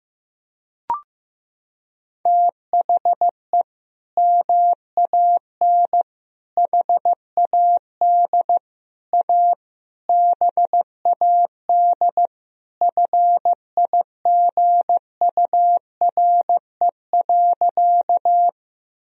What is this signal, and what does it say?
Morse code, heard as a keyed tone, about 15 wpm, 700 Hz. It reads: THE MAN HAD A BAD FIGURE.
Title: Sentences from Top 1000 Words - Review Part 2 - 15wpm